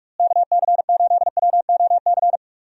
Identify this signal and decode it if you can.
Morse code, heard as a keyed tone, about 46 wpm, 700 Hz. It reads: KC8WOC
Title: USA Call Signs - 6 Characters - Repeated 2x - 45wpm